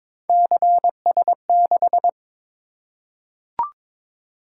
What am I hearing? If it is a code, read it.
Morse code, heard as a keyed tone, about 22 wpm, 700 Hz. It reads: CS6
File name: practice / 22wpm / Three Letter-Number Combinations - 22wpm